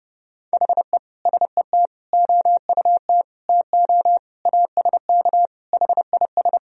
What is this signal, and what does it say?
Morse code, heard as a keyed tone, about 32 wpm, 700 Hz. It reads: HE SET OUT TO ASK HIS
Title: Sentences from Top 1000 Words - Review Part 1 - 30wpm